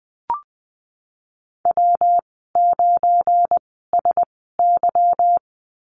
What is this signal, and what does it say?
Morse code, heard as a keyed tone, about 20 wpm, spaced as for 14 wpm, 700 Hz. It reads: W9SY
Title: 193 Lesson 31g - Letter v - Call Signs 20wpm